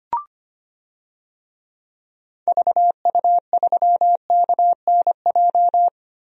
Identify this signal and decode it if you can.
Morse code, heard as a keyed tone, about 25 wpm, 700 Hz. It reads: VU3KNJ
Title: DXCC-100 Call Signs - 6 Characters - Repeated 2x - 25wpm